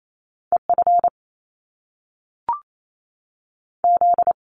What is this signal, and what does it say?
Morse code, heard as a keyed tone, about 28 wpm, 700 Hz. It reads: E F Z
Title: Top Two Letter Combinations in Words 28wpm